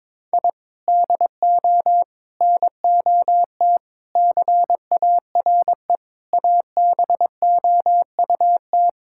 Morse code, heard as a keyed tone, about 22 wpm, 700 Hz. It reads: I DO NOT CARE ABOUT
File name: practice / 22wpm / Sentences with 4-Word Phrases - 22wpm